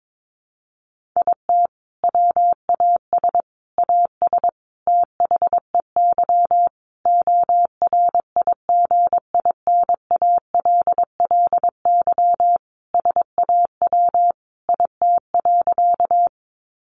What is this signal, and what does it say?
Morse code, heard as a keyed tone, about 22 wpm, 700 Hz. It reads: IT WAS AS THEY ORIGINALLY SAW IT.
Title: Sentences from Top 1500 Words 22wpm